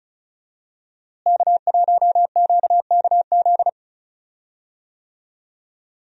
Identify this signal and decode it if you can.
Morse code, heard as a keyed tone, about 34 wpm, 700 Hz. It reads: K1QKZ
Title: USA Call Signs - 5 Characters - Repeated 2x - 35wpm